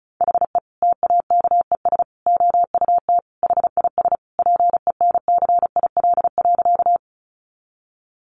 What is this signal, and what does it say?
Morse code, heard as a keyed tone, about 34 wpm, 700 Hz. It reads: HE TAKES OUT HIS PENCIL.
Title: Additional Easy Phrases 35wpm